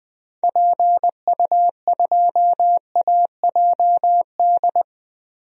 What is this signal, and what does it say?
Morse code, heard as a keyed tone, about 20 wpm, 700 Hz. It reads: PU2AJD
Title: DXCC-100 Call Signs - 6 Characters - Repeated 3x - 20wpm